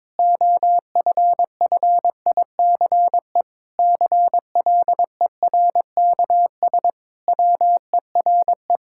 Morse code, heard as a keyed tone, about 22 wpm, 700 Hz. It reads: OFFICE CLERKS WERE
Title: Pangram Challenge v2 - 22wpm